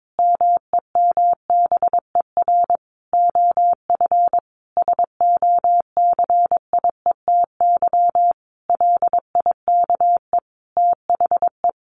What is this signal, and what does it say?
Morse code, heard as a keyed tone, about 22 wpm, 700 Hz. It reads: MEMBER OF SOCIETY LIKE THE
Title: Sentences from Top 1100 Words 22wpm